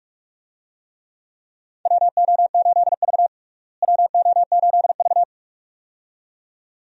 Morse code, heard as a keyed tone, about 43 wpm, 700 Hz. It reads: WO8V WO8V
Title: USA Call Signs - 4 Characters - Repeated 2x - 45wpm